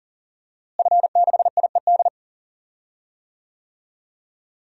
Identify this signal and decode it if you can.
Morse code, heard as a keyed tone, about 40 wpm, 700 Hz. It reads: F6IED
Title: DXCC-100 Call Signs - 5 Characters - Repeated 2x - 40wpm